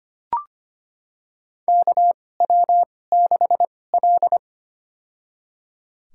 Morse code, heard as a keyed tone, about 25 wpm, spaced as for 18 wpm, 700 Hz. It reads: KW6L
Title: USA Call Signs - 2x Character Spacing - 25wpm